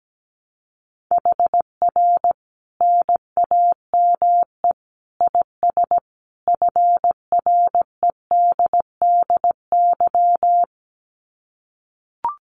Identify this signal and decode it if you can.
Morse code, heard as a keyed tone, about 17 wpm, 700 Hz. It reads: HR NAME IS FREDDY
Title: Common QSO Phrases 17wpm